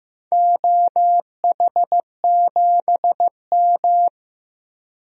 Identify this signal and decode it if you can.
Morse code, heard as a keyed tone, about 16 wpm, 700 Hz. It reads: OH7M
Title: DXCC-100 Call Signs - 4 Characters - 15wpm